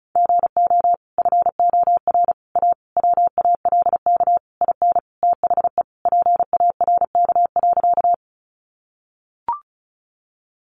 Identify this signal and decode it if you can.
Morse code, heard as a keyed tone, about 34 wpm, 700 Hz. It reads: GO FOR A WALK IN THE PARK.
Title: Sentences from Top 1300 Words 35wpm